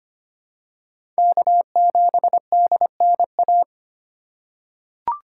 Morse code, heard as a keyed tone, about 25 wpm, 700 Hz. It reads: K7DNA
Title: USA Call Signs - 5 Characters - 6x Character Spacing - 25wpm